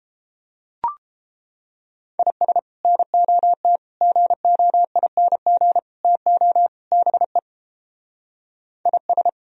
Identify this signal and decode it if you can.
Morse code, heard as a keyed tone, about 31 wpm, 700 Hz. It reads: IS NOT GOING TO BE IS